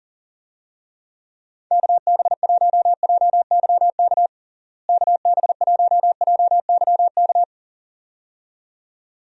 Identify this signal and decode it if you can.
Morse code, heard as a keyed tone, about 40 wpm, 700 Hz. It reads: KB1JYK KB1JYK
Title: USA Call Signs - 6 Characters - Repeated 2x - 40wpm